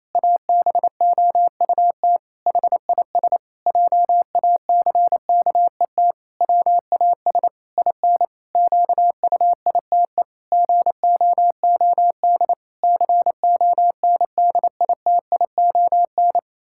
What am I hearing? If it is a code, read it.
Morse code, heard as a keyed tone, about 28 wpm, 700 Hz. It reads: ABOUT HIS JACKET WAS IN QUITE GOOD CONDITION
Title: Pangram Challenge v2 - 28wpm